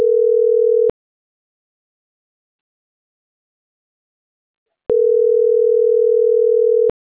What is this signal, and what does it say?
Ringback tone (North America), 440 + 480 Hz.